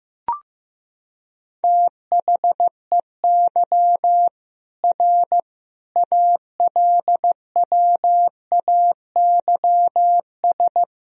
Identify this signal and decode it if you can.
Morse code, heard as a keyed tone, about 15 wpm, 700 Hz. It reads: THEY R ALWAYS
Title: Sentences from Top 300 Words and Abbreviations 15wpm